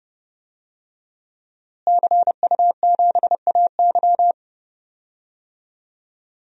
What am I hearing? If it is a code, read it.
Morse code, heard as a keyed tone, about 30 wpm, 700 Hz. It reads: CU7AY
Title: DXCC-100 Call Signs - 5 Characters - Repeated 3x - 30wpm